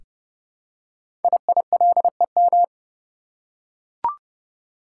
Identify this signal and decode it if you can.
Morse code, heard as a keyed tone, about 30 wpm, 700 Hz. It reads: IILEM
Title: Code Groups - 5 Letter:Numbers - 30wpm